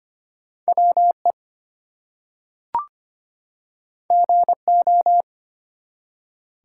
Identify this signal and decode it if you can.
Morse code, heard as a keyed tone, about 25 wpm, 700 Hz. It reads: WE GO
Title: Two Letter Words - 25wpm